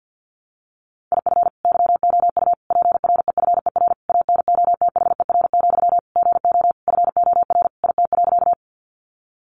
Morse code, heard as a keyed tone, about 50 wpm, 700 Hz. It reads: IF YOU PREFER ANOTHER, GO FOR IT.
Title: Sentences from Top 1100 Words 50wpm